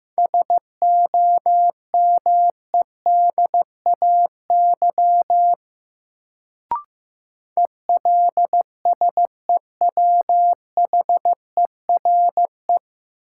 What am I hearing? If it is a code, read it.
Morse code, heard as a keyed tone, about 16 wpm, 700 Hz. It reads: SOMEDAY ELSEWHERE
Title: Sentences from Top 1600 Words - New Word Drill Alt - 15wpm